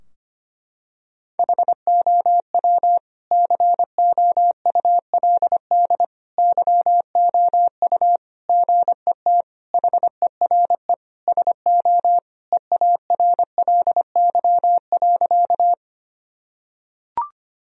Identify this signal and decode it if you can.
Morse code, heard as a keyed tone, about 25 wpm, 700 Hz. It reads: HOW COULD YOU GET HERE SO EARLY.
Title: Sentences from Top 1000 Words - Review Part 2 - 25wpm